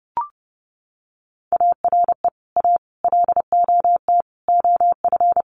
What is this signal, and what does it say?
Morse code, heard as a keyed tone, about 30 wpm, 700 Hz. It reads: ARE A LOT OF
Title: Sets of 5 Words Encore2 30wpm